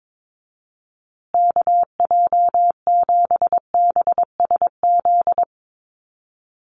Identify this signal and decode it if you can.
Morse code, heard as a keyed tone, about 22 wpm, 700 Hz. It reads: KJ7BSZ
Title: USA Call Signs Repeated 2x 22wpm